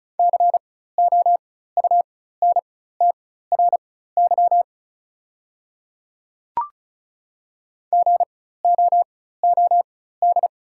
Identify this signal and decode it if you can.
Morse code, heard as a keyed tone, about 36 wpm, spaced as for 16 wpm, 700 Hz. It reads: COUNTRY GOOD
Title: 4x Character Spacing - Top 300 Words - 35wpm